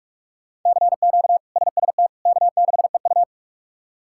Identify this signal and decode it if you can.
Morse code, heard as a keyed tone, about 41 wpm, 700 Hz. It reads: CQ SST K6EU
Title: K1USN SST QSO Practice - 45wpm